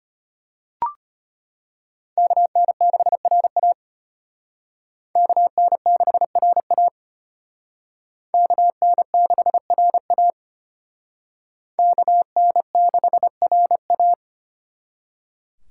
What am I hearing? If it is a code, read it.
Morse code, heard as a keyed tone, about 32 wpm, 700 Hz. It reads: KN6RA KN6RA KN6RA KN6RA